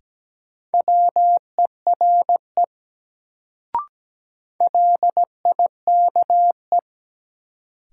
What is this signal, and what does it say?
Morse code, heard as a keyed tone, about 17 wpm, 700 Hz. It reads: WERE LIKE
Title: Sentences from Top 100 Words - New Word Drill - 17wpm